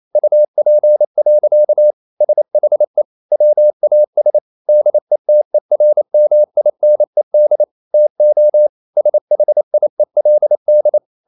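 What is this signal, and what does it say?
Morse code, heard as a keyed tone, about 28 wpm, 600 Hz. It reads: UP. SHE WAS DETERMINED TO SHIELD
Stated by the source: Morse code practice at 28 wpm, text of Ken Follett's novel Never in English, from CW med Gnister.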